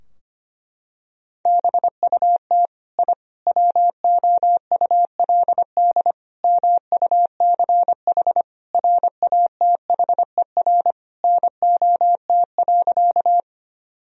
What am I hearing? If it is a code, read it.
Morse code, heard as a keyed tone, about 25 wpm, 700 Hz. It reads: BUT I WOULD MUCH RATHER NOT.
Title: Sentences from Top 1000 Words - Review Part 4 - 25wpm